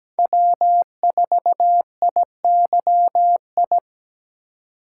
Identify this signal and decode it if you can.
Morse code, heard as a keyed tone, about 20 wpm, 700 Hz. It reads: W4IYI
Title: Speed Racing - USA Call Signs 17wpm